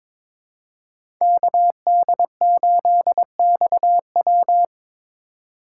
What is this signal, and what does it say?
Morse code, heard as a keyed tone, about 22 wpm, 700 Hz. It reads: KD8XW
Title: USA Call Signs - 5 Characters - 2x Character Spacing - 22wpm